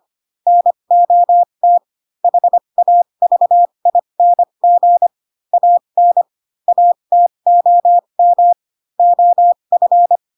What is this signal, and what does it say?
Morse code, heard as a keyed tone, about 25 wpm, spaced as for 22 wpm, 700 Hz. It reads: NOT HAVING AN ATOM OF
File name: Pinocchio / Pinocchio ch2 18 25 ik1hge